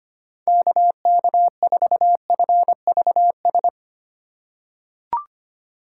Morse code, heard as a keyed tone, about 25 wpm, 700 Hz. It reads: KK4FVS
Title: USA Call Signs - 6 Characters - 3x Character Spacing - 25wpm